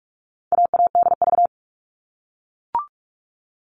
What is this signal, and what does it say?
Morse code, heard as a keyed tone, about 44 wpm, 700 Hz. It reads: AADV
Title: All Cut Numbers - 4-Digit 45wpm